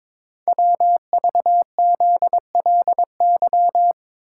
Morse code, heard as a keyed tone, about 22 wpm, 700 Hz. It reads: WVZLY
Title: Code Groups - 5 Letters - 22wpm